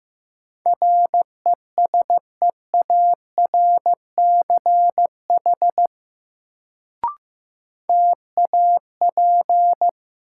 Morse code, heard as a keyed tone, about 15 wpm, 700 Hz. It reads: RESEARCH TAP
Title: Sentences from Top 1400 Words - New Word Drill Alt - 15wpm